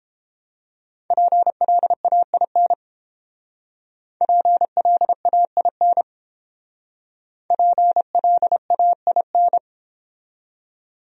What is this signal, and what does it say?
Morse code, heard as a keyed tone, about 29 wpm, 700 Hz. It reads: PLAIN PLAIN PLAIN